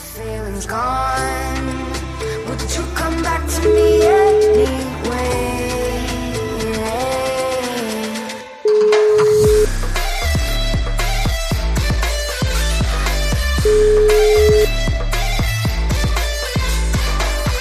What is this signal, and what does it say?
Ringback tone (Europe, 425 Hz), with noise.